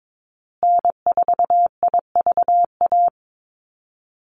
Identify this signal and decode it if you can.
Morse code, heard as a keyed tone, about 22 wpm, 700 Hz. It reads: N4IVA
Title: USA Call Signs Repeated 2x 22wpm